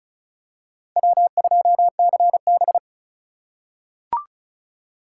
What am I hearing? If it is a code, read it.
Morse code, heard as a keyed tone, about 36 wpm, 700 Hz. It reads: W2CB
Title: USA Call Signs - 3x Character Spacing - 35wpm